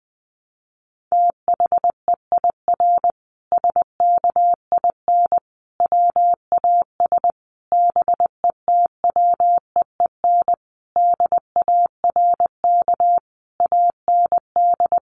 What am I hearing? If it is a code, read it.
Morse code, heard as a keyed tone, about 20 wpm, 700 Hz. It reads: THEIR SKIN WAS BETWEEN DARK AND
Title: Sentences from Top 1300 Words 20wpm